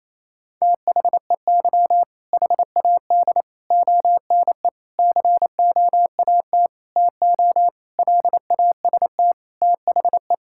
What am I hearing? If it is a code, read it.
Morse code, heard as a keyed tone, about 28 wpm, 700 Hz. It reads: THEY HAD ONE COAT TO LAST THE
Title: Sentences from Top 1600 Words 28wpm